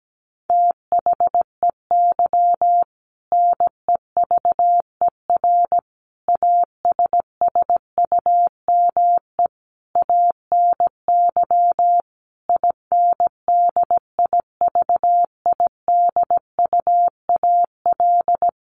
Morse code, heard as a keyed tone, about 17 wpm, 700 Hz. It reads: THEY NEVER ASSUME ANY INDIVIDUAL
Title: Sentences from Top 1500 Words 17wpm